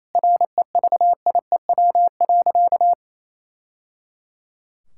Morse code, heard as a keyed tone, about 28 wpm, 700 Hz. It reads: REVIEW.
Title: Sentences from Top 1000 Words - Review Part 6 - 28wpm